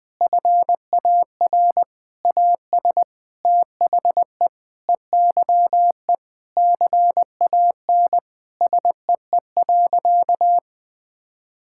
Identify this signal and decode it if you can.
Morse code, heard as a keyed tone, about 20 wpm, 700 Hz. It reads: FAR AS THE EYE CAN SEE.
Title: Sentences with 7-Word Phrases - Spoken Beginning - 20wpm